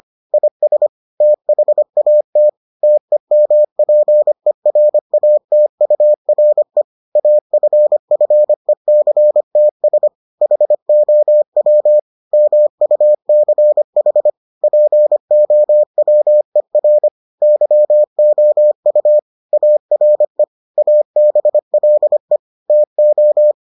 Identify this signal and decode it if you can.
Morse code, heard as a keyed tone, about 25 wpm, 600 Hz. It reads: IS THAT TEMPERATURE AFFECTS HOW MUCH POWER YOU ARE ABLE TO